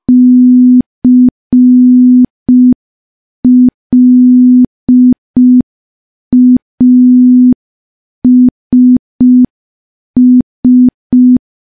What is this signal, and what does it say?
Morse code, heard as a keyed tone, about 5 wpm, 250 Hz. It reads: CLASS
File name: morse / morse-training-05WPM